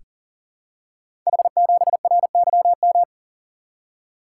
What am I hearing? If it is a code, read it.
Morse code, heard as a keyed tone, about 40 wpm, 690 Hz. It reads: H7RYM
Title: Code Groups - 5 Letter:Numbers - 40wpm